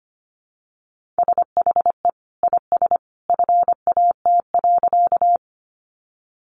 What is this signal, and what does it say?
Morse code, heard as a keyed tone, about 25 wpm, 700 Hz. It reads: SHE IS FAT.
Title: Sentences from Top 600 Words 25wpm